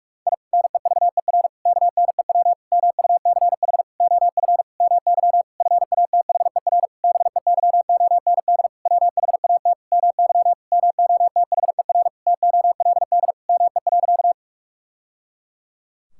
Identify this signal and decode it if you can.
Morse code, heard as a keyed tone, about 41 wpm, 700 Hz. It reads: I NEVER KNEW MUCH OF MY FATHER BEYOND WHAT MY MOTHER TOLD ME.
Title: Sentences from Top 1000 Words - Review Part 6 - 45wpm